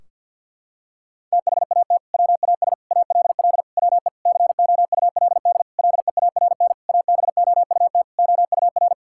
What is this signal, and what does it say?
Morse code, heard as a keyed tone, about 51 wpm, 700 Hz. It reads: THAT WAS ALL WE COULD LEARN ABOUT OUR
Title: Sentences from Top 1600 Words 50wpm